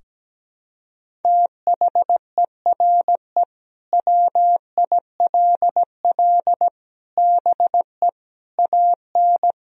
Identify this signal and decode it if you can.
Morse code, heard as a keyed tone, about 17 wpm, 700 Hz. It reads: THERE WILL BE AN
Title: Sentences from Top 1000 Words - Review Part 4 - 17wpm